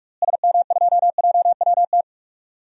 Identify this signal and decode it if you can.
Morse code, heard as a keyed tone, about 46 wpm, 700 Hz. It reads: SM2JWT
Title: DXCC-100 Call Signs - 6 Characters - Repeated 2x - 45wpm